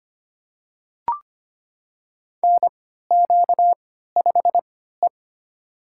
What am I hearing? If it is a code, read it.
Morse code, heard as a keyed tone, about 25 wpm, spaced as for 14 wpm, 700 Hz. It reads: NQ5E
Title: USA Call Signs - 4 Characters - 3x Character Spacing - 25wpm